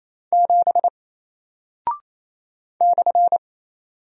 Morse code, heard as a keyed tone, about 28 wpm, 700 Hz. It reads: 7 /
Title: Single Character (Letter:Number:Punctuation:Prosign) Alt 28wpm